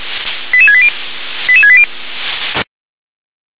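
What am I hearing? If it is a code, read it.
Five-tone selective call (ZVEI), heard as quick runs of five tones, tones of about 70 ms. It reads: 88680 88680
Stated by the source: ZVEI five-tone call received by SDR.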